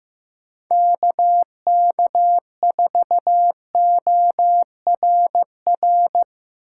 Morse code, heard as a keyed tone, about 15 wpm, 700 Hz. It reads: KK4ORR
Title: USA Call Signs - 6 Characters - 3x Character Spacing - 15wpm